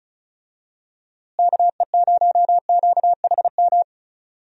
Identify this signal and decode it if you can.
Morse code, heard as a keyed tone, about 36 wpm, 700 Hz. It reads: KE0QHM
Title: USA Call Signs - 6 Characters - Repeated 2x - 35wpm